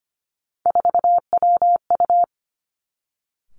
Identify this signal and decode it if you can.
Morse code, heard as a keyed tone, about 25 wpm, 700 Hz. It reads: 4WU
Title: Three Challenging Letter-Number Combinations 25wpm